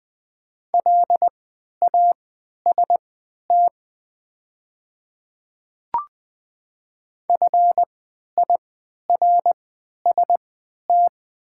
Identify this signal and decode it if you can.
Morse code, heard as a keyed tone, about 20 wpm, spaced as for 11 wpm, 700 Hz. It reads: LAST FIRST
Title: 3x Character Spacing - Top 300 Words - 20wpm